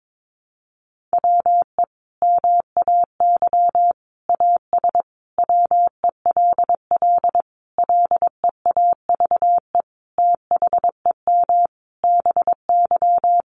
Morse code, heard as a keyed tone, about 22 wpm, 700 Hz. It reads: WE MAY AS WELL LEAVE THEM BY
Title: Sentences from Top 800 Words 22wpm